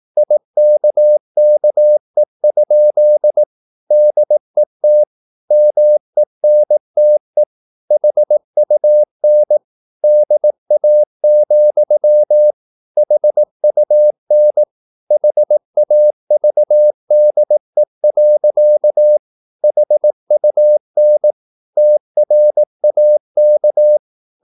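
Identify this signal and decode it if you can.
Morse code, heard as a keyed tone, about 18 wpm, 600 Hz. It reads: IKKE? DET MENTE HUN DA, HUN HAVDE. HUN TRAK
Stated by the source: Slow (QRS) Morse practice at 18 wpm, CW med Gnister.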